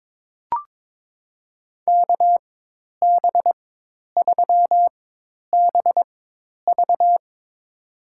Morse code, heard as a keyed tone, about 22 wpm, spaced as for 10 wpm, 700 Hz. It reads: KB3BV
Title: USA Call Signs - 5 Characters - 4x Character Spacing - 22wpm